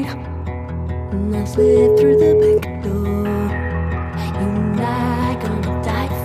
Ringback tone (Europe, 425 Hz), with noise.